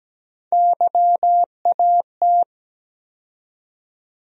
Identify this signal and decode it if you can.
Morse code, heard as a keyed tone, about 17 wpm, 700 Hz. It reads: YAT
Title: USA Call Sign Postfixes - 3 Letters 17wpm